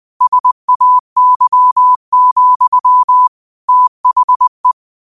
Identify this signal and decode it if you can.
Morse code, heard as a keyed tone, about 20 wpm, 1000 Hz. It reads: SAY, THE
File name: qotdfiles / 20wpm